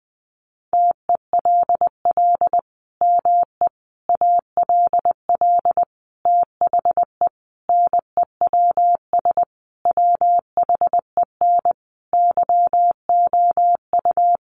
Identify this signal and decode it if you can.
Morse code, heard as a keyed tone, about 19 wpm, 700 Hz. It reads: TELL ME ALL THE NEWS WHEN YOU
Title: Sentences from Top 800 Words 20wpm